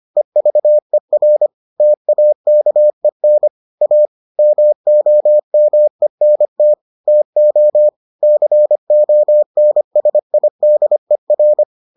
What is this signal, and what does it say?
Morse code, heard as a keyed tone, about 25 wpm, 600 Hz. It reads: EVER TAKEN A MOMENT TO CONSIDER